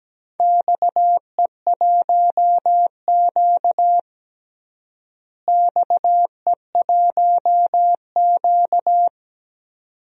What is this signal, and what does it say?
Morse code, heard as a keyed tone, about 17 wpm, 700 Hz. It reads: XE1Q XE1Q
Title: DXCC-100 Call Signs - 4 Characters - Repeated 2x - 17wpm